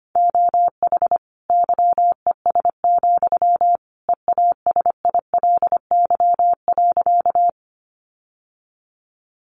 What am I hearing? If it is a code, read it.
Morse code, heard as a keyed tone, about 25 wpm, 700 Hz. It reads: OH YES, EASILY.
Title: Sentences from Top 900 Words 25wpm